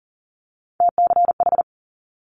Morse code, heard as a keyed tone, about 39 wpm, 700 Hz. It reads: TCH
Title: Multi-letter Phonemes Quick Fire 40wpm